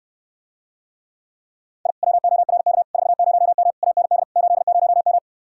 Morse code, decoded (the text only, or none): I WORK 40M AND 80M